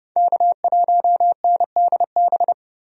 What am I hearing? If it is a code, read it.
Morse code, heard as a keyed tone, about 32 wpm, 700 Hz. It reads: K1NDB
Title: USA Call Signs - 5 Characters - 30wpm